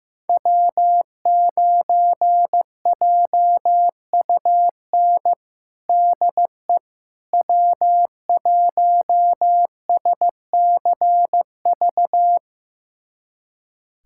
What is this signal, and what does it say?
Morse code, heard as a keyed tone, about 16 wpm, 700 Hz. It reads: W9JUN DE W1SCV
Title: USA Callsign DE Callsign - 5 Characters - 15wpm